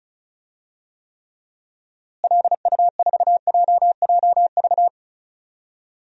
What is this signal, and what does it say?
Morse code, heard as a keyed tone, about 34 wpm, 700 Hz. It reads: LU4JJV
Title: DXCC-100 Call Signs - 6 Characters - Repeated 3x - 35wpm